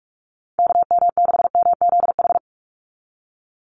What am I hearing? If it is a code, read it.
Morse code, heard as a keyed tone, about 44 wpm, 700 Hz. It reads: KM6MZH
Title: USA Call Signs 45wpm